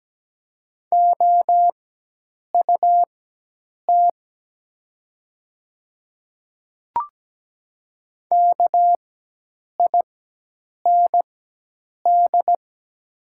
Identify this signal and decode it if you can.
Morse code, heard as a keyed tone, about 17 wpm, spaced as for 8 wpm, 700 Hz. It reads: OUT KIND
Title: 4x Character Spacing - Top 300 Words - 17wpm